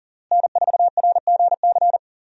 Morse code, heard as a keyed tone, about 40 wpm, 700 Hz. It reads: N4RGC
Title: USA Call Signs - 5 Characters - Repeated 3x - 40wpm